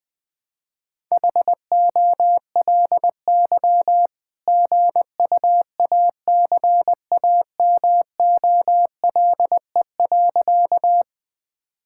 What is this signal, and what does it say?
Morse code, heard as a keyed tone, about 20 wpm, 700 Hz. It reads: HOLY GUACAMOLE.
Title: All-Age Appropriate Jokes - 20wpm